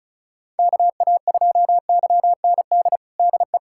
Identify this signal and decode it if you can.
Morse code, heard as a keyed tone, about 36 wpm, 700 Hz. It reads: KA2YND DE
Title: USA Callsign DE Callsign - 6 Characters - 35wpm